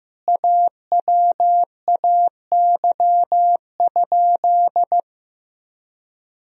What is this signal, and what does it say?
Morse code, heard as a keyed tone, about 15 wpm, 700 Hz. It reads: AWAY?
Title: Sentences from Top 1000 Words - Review Part 1 - 15wpm